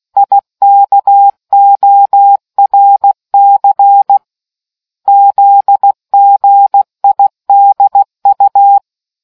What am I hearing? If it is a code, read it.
Morse code, heard as a keyed tone, about 16 wpm, 800 Hz. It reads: IKORC ZGIDU